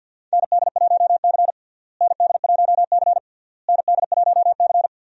Morse code, heard as a keyed tone, about 48 wpm, 700 Hz. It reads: ND1C ND1C ND1C
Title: USA Call Signs - 4 Characters - Repeated 3x - 50wpm